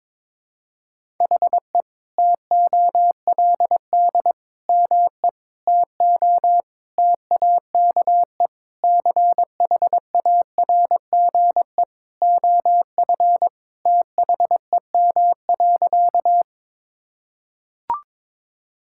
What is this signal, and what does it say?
Morse code, heard as a keyed tone, about 22 wpm, 700 Hz. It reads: HE TOLD ME TO TAKE CHARGE OF THEM.
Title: Sentences from Top 1000 Words - Review Part 6 - 22wpm